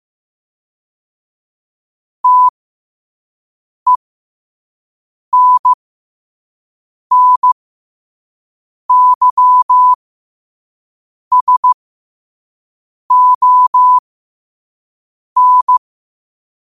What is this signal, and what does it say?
Morse code, heard as a keyed tone, about 15 wpm, spaced as for 5 wpm, 1000 Hz. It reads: TENNYSON
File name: qotdfiles / 5wpm